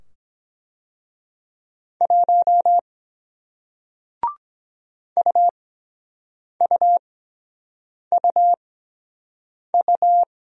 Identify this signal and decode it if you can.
Morse code, heard as a keyed tone, about 23 wpm, spaced as for 7 wpm, 700 Hz. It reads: 1 UUUU